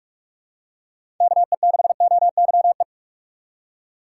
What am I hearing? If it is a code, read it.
Morse code, heard as a keyed tone, about 41 wpm, 700 Hz. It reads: KE6OYE
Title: USA Call Signs - 6 Characters - 2x Character Spacing - 45wpm